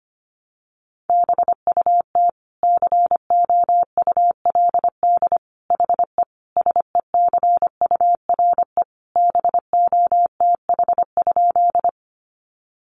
Morse code, heard as a keyed tone, about 25 wpm, 700 Hz. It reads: BUT COULD HE SECURE BOTH?
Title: Sentences from Top 1200 Words 25wpm